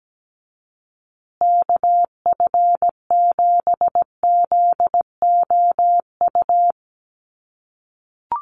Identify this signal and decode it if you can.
Morse code, heard as a keyed tone, about 17 wpm, 700 Hz. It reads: KF7ZOU
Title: USA Call Signs Repeated 3x 17wpm